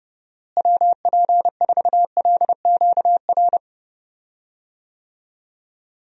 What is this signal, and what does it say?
Morse code, heard as a keyed tone, about 30 wpm, 700 Hz. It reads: WP4LQR
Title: USA Call Signs - 6 Characters - Repeated 3x - 30wpm